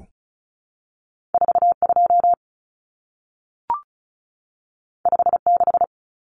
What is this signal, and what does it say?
Morse code, heard as a keyed tone, about 36 wpm, 700 Hz. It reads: 42 56
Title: Numbers - 2-Digit 35wpm